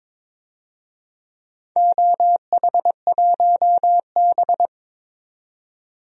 Morse code, heard as a keyed tone, about 22 wpm, 700 Hz. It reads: OH1B
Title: DXCC-100 Call Signs - 4 Characters - 22wpm